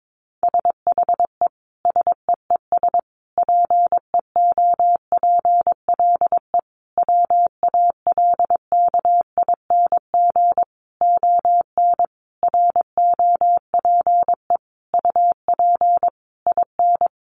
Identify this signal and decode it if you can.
Morse code, heard as a keyed tone, about 22 wpm, 700 Hz. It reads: SHE SEES PEOPLE WALKING ON ROPE UP IN
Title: Easy Phrases 22wpm